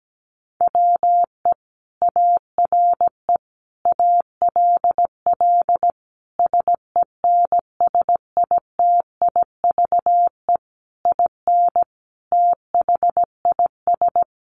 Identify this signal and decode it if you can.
Morse code, heard as a keyed tone, about 17 wpm, 700 Hz. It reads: WE ARE ALL SENSITIVE IN THIS
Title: Sentences from Top 1400 Words 17wpm